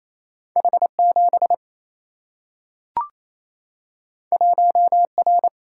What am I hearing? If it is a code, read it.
Morse code, heard as a keyed tone, about 28 wpm, 700 Hz. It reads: H 7 1 R
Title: Single-Letter Single-Number 28wpm